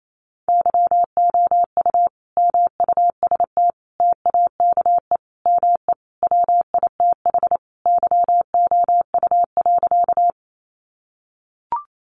Morse code, heard as a keyed tone, about 28 wpm, 700 Hz. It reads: YOU MUST TAKE ME WITH YOU.
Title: Sentences from Top 200 Words 28wpm